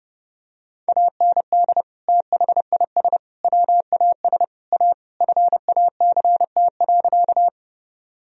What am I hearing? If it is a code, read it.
Morse code, heard as a keyed tone, about 30 wpm, 700 Hz. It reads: AND THIS WAS A FACT.
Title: Sentences from Top 1000 Words - Review Part 3 - 30wpm